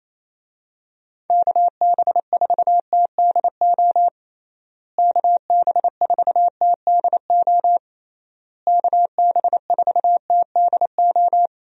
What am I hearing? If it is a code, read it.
Morse code, heard as a keyed tone, about 27 wpm, 700 Hz. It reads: KB4TDO KB4TDO KB4TDO
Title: USA Call Signs - 6 Characters - Repeated 3x - 28wpm